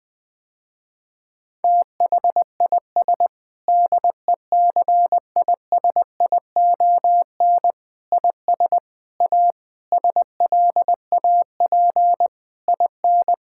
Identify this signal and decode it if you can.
Morse code, heard as a keyed tone, about 20 wpm, 700 Hz. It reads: THIS DECISION IS A SLAP IN
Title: Sentences with 8-Word Phrases - Spoken Beginning - 20wpm